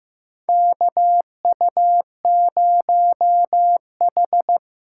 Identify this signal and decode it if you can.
Morse code, heard as a keyed tone, about 15 wpm, 700 Hz. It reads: KU0H
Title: USA Callsign DE Callsign - 4 Characters - 15wpm